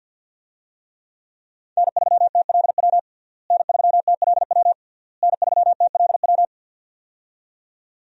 Morse code, heard as a keyed tone, about 51 wpm, 700 Hz. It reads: N3TLW N3TLW N3TLW
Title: USA Call Signs - 5 Characters - Repeated 3x - 50wpm